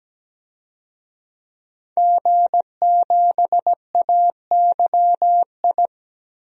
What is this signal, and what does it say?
Morse code, heard as a keyed tone, about 17 wpm, 700 Hz. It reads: G7AYI
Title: DXCC-100 Call Signs - 5 Characters - 17wpm